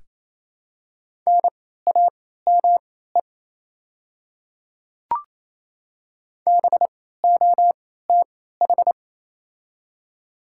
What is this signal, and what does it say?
Morse code, heard as a keyed tone, about 28 wpm, spaced as for 16 wpm, 700 Hz. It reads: NAME BOTH
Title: 3x Character Spacing - Top 300 Words - 28wpm